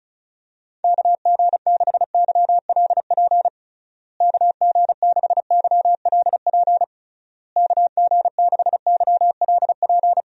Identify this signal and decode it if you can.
Morse code, heard as a keyed tone, about 34 wpm, 700 Hz. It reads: KG6YLP KG6YLP KG6YLP
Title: USA Call Signs - 6 Characters - Repeated 3x - 35wpm